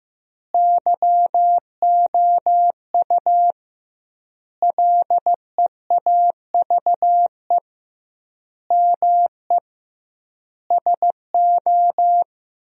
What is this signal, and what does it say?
Morse code, heard as a keyed tone, about 16 wpm, 700 Hz. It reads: YOU LEAVE ME SO
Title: 2x Word-Spacing - Sentences from Top 400 Words - 15wpm